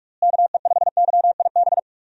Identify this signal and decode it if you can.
Morse code, heard as a keyed tone, about 44 wpm, 700 Hz. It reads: KE5YIB
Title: USA Call Signs - 6 Characters - Repeated 3x - 45wpm